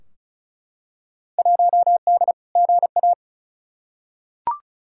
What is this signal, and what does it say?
Morse code, heard as a keyed tone, about 36 wpm, 700 Hz. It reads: 1D GA
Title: Field Day Exchanges - 35wpm